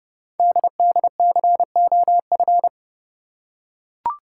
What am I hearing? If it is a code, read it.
Morse code, heard as a keyed tone, about 30 wpm, 700 Hz. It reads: DDCOF
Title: Code Groups - 5 Letters - 30wpm